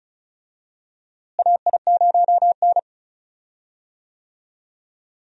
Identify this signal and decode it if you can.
Morse code, heard as a keyed tone, about 34 wpm, 700 Hz. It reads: AI0N
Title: USA Call Signs - 4 Characters - Repeated 2x - 35wpm